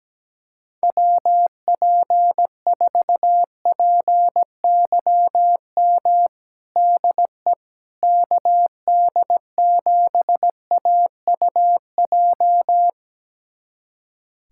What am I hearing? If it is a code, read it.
Morse code, heard as a keyed tone, about 17 wpm, 700 Hz. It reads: WP4PYM DE KD7AUJ
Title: USA Callsign DE Callsign - 6 Characters - 17wpm